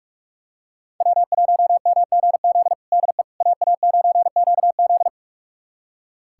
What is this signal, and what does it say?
Morse code, heard as a keyed tone, about 41 wpm, 700 Hz. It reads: W1MGZ DE AA9QZ
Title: USA Callsign DE Callsign - 5 Characters - 45wpm